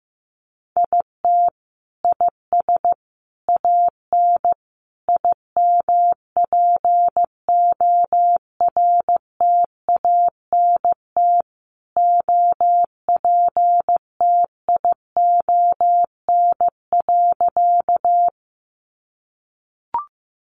Morse code, heard as a keyed tone, about 15 wpm, 700 Hz. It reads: IT IS AN IMPORTANT OPTION.
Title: Sentences from Top 1300 Words 15wpm